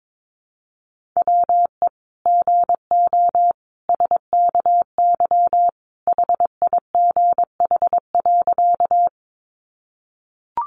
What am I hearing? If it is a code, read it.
Morse code, heard as a keyed tone, about 22 wpm, 700 Hz. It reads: WE GO SKY HIGH.
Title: Sentences from Top 1200 Words 22wpm